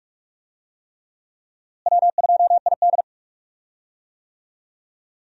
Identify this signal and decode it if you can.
Morse code, heard as a keyed tone, about 42 wpm, 700 Hz. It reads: W2ID
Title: USA Call Signs - 4 Characters - Repeated 3x - 45wpm